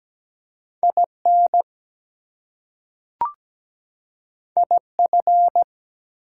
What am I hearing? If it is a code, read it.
Morse code, heard as a keyed tone, about 17 wpm, 700 Hz. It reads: IN IF
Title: Two Letter Words - 17wpm